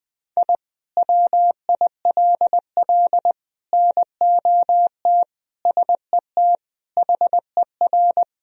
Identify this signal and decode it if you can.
Morse code, heard as a keyed tone, about 21 wpm, 700 Hz. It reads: I WILL NOT SET HER
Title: Sentences from Top 1000 Words - Review Part 1 - 20wpm